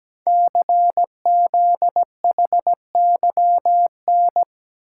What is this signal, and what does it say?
Morse code, heard as a keyed tone, about 17 wpm, 700 Hz. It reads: CZHYN
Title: Code Groups - 5 Letters - 17wpm